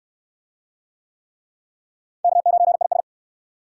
Morse code, heard as a keyed tone, about 66 wpm, 700 Hz. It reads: K9IS